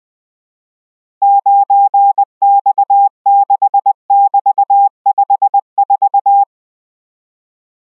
Morse code, heard as a keyed tone, about 20 wpm, 800 Hz. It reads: 9X6<BT>54